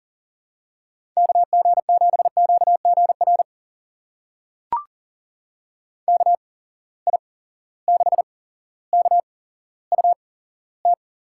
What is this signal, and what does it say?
Morse code, heard as a keyed tone, about 38 wpm, spaced as for 17 wpm, 700 Hz. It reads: KG7QGR KI6KUT
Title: USA Call Signs - 6 Characters - 8x Character Spacing - 40wpm